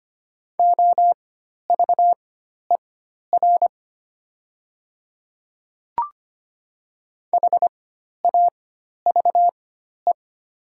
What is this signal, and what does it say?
Morse code, heard as a keyed tone, about 25 wpm, spaced as for 12 wpm, 700 Hz. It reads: OVER HAVE
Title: 4x Character Spacing - Top 100 Words - 25wpm